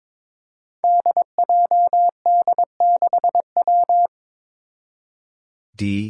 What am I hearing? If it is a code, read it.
Morse code, heard as a keyed tone, about 21 wpm, 700 Hz. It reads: DJD6W
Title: Five Challenging Letter-Number Combinations 22wpm